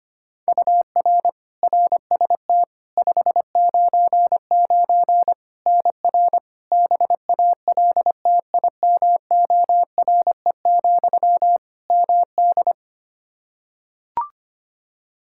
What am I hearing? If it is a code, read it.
Morse code, heard as a keyed tone, about 25 wpm, 700 Hz. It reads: UR RST 599 NR BALTIMORE, MD
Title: Practice QSO - Exchange 1 - Beginning - 25wpm